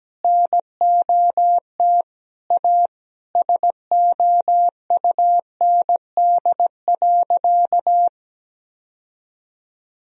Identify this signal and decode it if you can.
Morse code, heard as a keyed tone, about 17 wpm, 700 Hz. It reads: NOT A SOUND.
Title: Sentences from Top 1000 Words - Review Part 1 - 17wpm